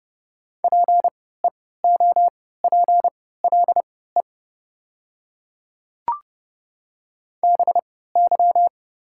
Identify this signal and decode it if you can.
Morse code, heard as a keyed tone, about 30 wpm, spaced as for 17 wpm, 700 Hz. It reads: PEOPLE BY
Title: 3x Character Spacing - Top 100 Words - 30wpm